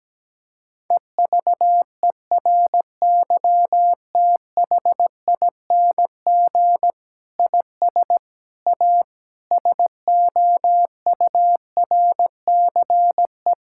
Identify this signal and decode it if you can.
Morse code, heard as a keyed tone, about 17 wpm, 700 Hz. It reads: EVERYTHING IS A SOURCE
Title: Sentences from Top 1000 Words - Review Part 5 - 17wpm